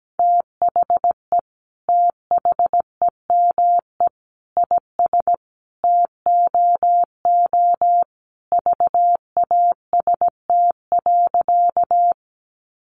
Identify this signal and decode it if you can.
Morse code, heard as a keyed tone, about 17 wpm, 700 Hz. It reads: THE THEME IS TOO VAST.
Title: Sentences from Top 1300 Words 17wpm